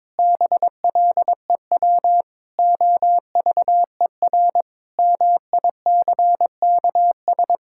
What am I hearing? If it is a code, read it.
Morse code, heard as a keyed tone, about 22 wpm, 700 Hz. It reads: BLEW OVER MICKS
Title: Pangram Challenge v2 - 22wpm